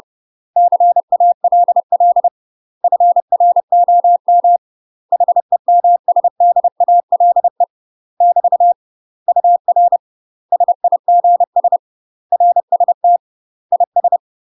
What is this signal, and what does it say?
Morse code, heard as a keyed tone, about 30 wpm, 700 Hz. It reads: CALL FROM HEMSDALE <BT> UR SIGS RST IS
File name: UK MASTER QSOs TRAINING / 30WPM / UK MASTER QSOs 0014